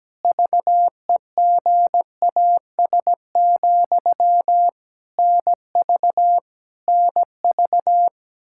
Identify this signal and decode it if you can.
Morse code, heard as a keyed tone, about 17 wpm, 700 Hz. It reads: VEGAS, NV NV
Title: K1USN SST QSO Practice 17wpm